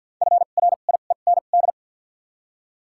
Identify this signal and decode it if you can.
Morse code, heard as a keyed tone, about 51 wpm, 700 Hz. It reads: F R I E N D
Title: 2x Character Spacing - Top 500 Words - 50wpm